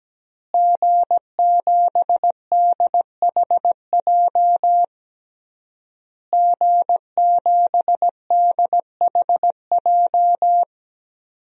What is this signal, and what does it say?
Morse code, heard as a keyed tone, about 17 wpm, 700 Hz. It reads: G7DHJ G7DHJ
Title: DXCC-100 Call Signs - 5 Characters - Repeated 3x - 17wpm